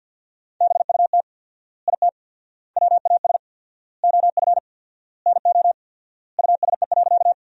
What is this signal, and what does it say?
Morse code, heard as a keyed tone, about 50 wpm, 700 Hz. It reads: BUT IT WAS OF NO USE.
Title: 4x Word-Spacing - Sentences from Top 100 Words - 50wpm